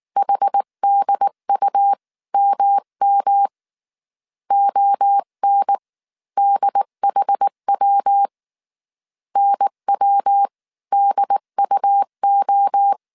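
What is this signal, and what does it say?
Morse code, heard as a keyed tone, about 19 wpm, spaced as for 16 wpm, 800 Hz. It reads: HDUMM ONDHW NWDUO
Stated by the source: Morse code dictation at 18 wpm from the IW3SGT QRS Morse code training course, five-character groups of 300 characters.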